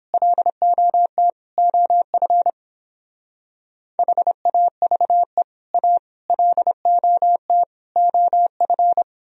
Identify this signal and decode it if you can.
Morse code, heard as a keyed tone, about 31 wpm, 700 Hz. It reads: LOT OF HAVE A LOT OF